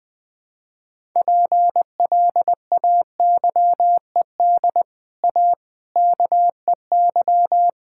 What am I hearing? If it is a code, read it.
Morse code, heard as a keyed tone, about 21 wpm, 700 Hz. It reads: PLAYED A KEY
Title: Sets of 6 Words - 20wpm